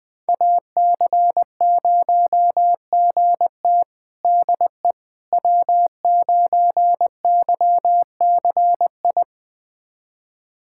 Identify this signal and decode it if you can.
Morse code, heard as a keyed tone, about 20 wpm, 700 Hz. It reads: AC0GT DE W9YCI
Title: USA Callsign DE Callsign - 5 Characters - 20wpm